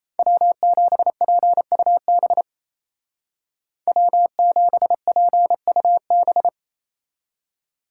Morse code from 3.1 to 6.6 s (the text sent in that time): W7PUB